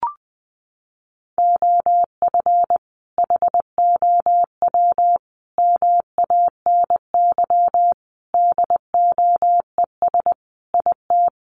Morse code, heard as a keyed tone, about 20 wpm, 700 Hz. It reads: OF HOW MANY DOES IT
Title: Sentences from Top 1400 Words 20wpm